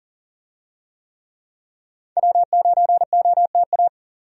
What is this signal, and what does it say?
Morse code, heard as a keyed tone, about 40 wpm, 700 Hz. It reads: W9OTA
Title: USA Call Signs - 5 Characters - 40wpm